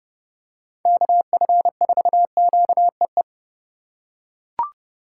Morse code, heard as a keyed tone, about 30 wpm, 700 Hz. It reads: KF4QEE
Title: USA Call Signs - 6 Characters - 4x Character Spacing - 30wpm